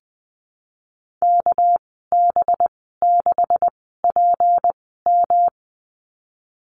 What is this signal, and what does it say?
Morse code, heard as a keyed tone, about 19 wpm, 700 Hz. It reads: K B 6 P M
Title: 200 Lesson 32g - Letter k - Call Signs 20wpm